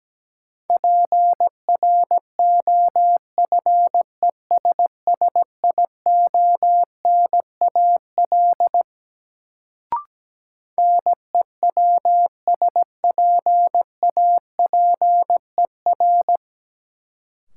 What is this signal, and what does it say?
Morse code, heard as a keyed tone, about 17 wpm, 700 Hz. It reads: PROFESSIONAL NEWSPAPER
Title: Sentences from Top 1000 Words - Review Part 4 - New Word Drill - 17wpm